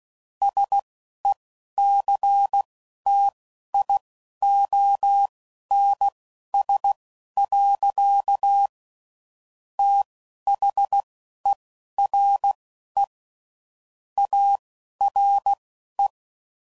Morse code, heard as a keyed tone, about 16 wpm, spaced as for 12 wpm, 800 Hz. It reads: SECTIONS. THERE ARE
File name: CW Code Groups @10WPM / Technical Subjects / SwPwrSup